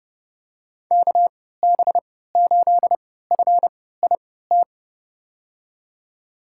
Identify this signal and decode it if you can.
Morse code, heard as a keyed tone, about 32 wpm, spaced as for 17 wpm, 700 Hz. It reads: KB8FIT
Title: USA Call Signs - 6 Characters - 3x Character Spacing - 30wpm